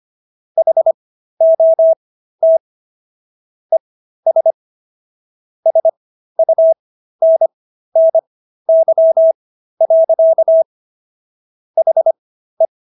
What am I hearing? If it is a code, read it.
Morse code, heard as a keyed tone, about 25 wpm, spaced as for 13 wpm, 650 Hz. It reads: HOT ES SUNNY. HE